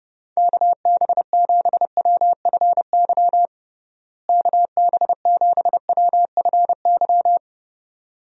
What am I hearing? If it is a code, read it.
Morse code, heard as a keyed tone, about 30 wpm, 700 Hz. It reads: KB7WFY KB7WFY
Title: USA Call Signs - 6 Characters - Repeated 2x - 30wpm